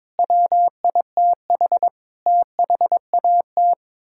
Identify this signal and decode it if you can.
Morse code, heard as a keyed tone, about 22 wpm, 700 Hz. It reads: WITH THAT
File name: practice / 22wpm / Sets of 7 Words - 22wpm